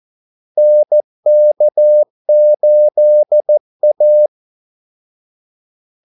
Morse code, heard as a keyed tone, about 14 wpm, 600 Hz. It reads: NK8A
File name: Callsigns-14wpm